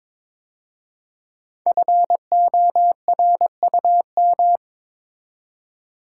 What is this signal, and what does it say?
Morse code, heard as a keyed tone, about 22 wpm, 700 Hz. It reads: FORUM
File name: 5 Letter Words - First 3 Letters Given - 22wpm